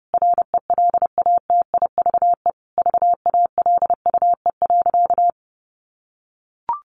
Morse code, heard as a keyed tone, about 30 wpm, 700 Hz. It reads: RELATIVE VALUE.
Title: Sentences from Top 1400 Words 30wpm